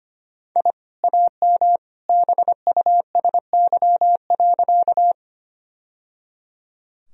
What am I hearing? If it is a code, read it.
Morse code, heard as a keyed tone, about 25 wpm, 700 Hz. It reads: I AM BUSY.
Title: Sentences from Top 1000 Words - Review Part 3 - 25wpm